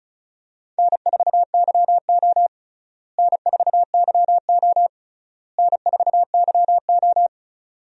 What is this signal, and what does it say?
Morse code, heard as a keyed tone, about 34 wpm, 700 Hz. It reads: N4YO N4YO N4YO
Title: USA Call Signs - 4 Characters - Repeated 3x - 35wpm